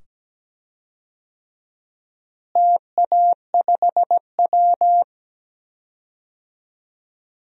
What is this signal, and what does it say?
Morse code, heard as a keyed tone, about 17 wpm, 700 Hz. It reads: TA5W
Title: DXCC-100 Call Signs - 4 Characters - Repeated 2x - 17wpm